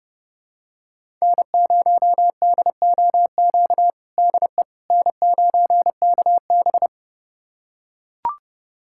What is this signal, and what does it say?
Morse code, heard as a keyed tone, about 32 wpm, 700 Hz. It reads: N0DOQ DE N9KB
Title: USA Callsign DE Callsign 30wpm